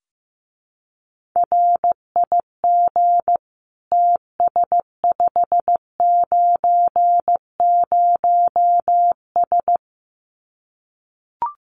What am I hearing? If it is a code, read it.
Morse code, heard as a keyed tone, about 15 wpm, 700 Hz. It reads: RIG TS590S
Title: Common QSO Phrases Encore 15wpm